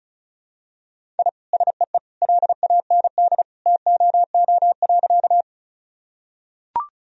Morse code, heard as a keyed tone, about 36 wpm, 700 Hz. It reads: I SEE LAND TOO.
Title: Sentences from Top 1000 Words - Review Part 1 - 35wpm